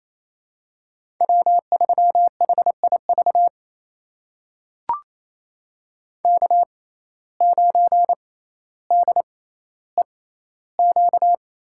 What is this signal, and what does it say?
Morse code, heard as a keyed tone, about 28 wpm, spaced as for 14 wpm, 700 Hz. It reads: W3HIV K9DEQ
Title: USA Call Signs - 5 Characters - 6x Character Spacing - 28wpm